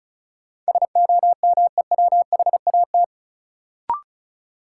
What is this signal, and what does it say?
Morse code, heard as a keyed tone, about 34 wpm, 700 Hz. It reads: SOMEWHAT
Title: Sentences from Top 1300 Words - New Word Drill Alt - 35wpm